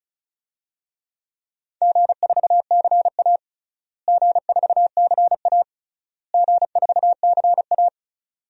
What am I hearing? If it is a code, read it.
Morse code, heard as a keyed tone, about 36 wpm, 700 Hz. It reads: G4CA G4CA G4CA
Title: DXCC-100 Call Signs - 4 Characters - Repeated 3x - 35wpm